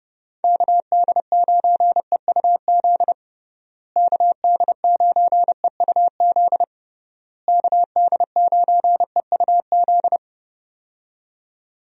Morse code, heard as a keyed tone, about 30 wpm, 700 Hz. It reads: KD9EUZ KD9EUZ KD9EUZ
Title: USA Call Signs - 6 Characters - Repeated 3x - 30wpm